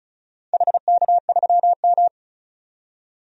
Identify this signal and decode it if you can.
Morse code, heard as a keyed tone, about 34 wpm, 700 Hz. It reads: HK3M
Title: DXCC-100 Call Signs - 4 Characters - 35wpm